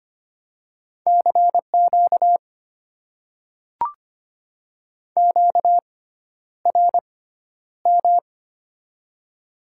Morse code, heard as a keyed tone, about 25 wpm, spaced as for 11 wpm, 700 Hz. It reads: CQ QRM
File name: Top 90 QSO Elements - 6x Character Spacing - 25wpm